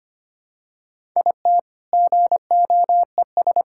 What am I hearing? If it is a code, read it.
Morse code, heard as a keyed tone, about 25 wpm, 700 Hz. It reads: IT GOES
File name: practice / 25wpm / All-Age Appropriate Jokes - 25wpm